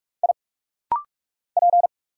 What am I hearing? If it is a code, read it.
Morse code, heard as a keyed tone, about 46 wpm, 700 Hz. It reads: I P
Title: Single Letters - Rapid-Fire 45wpm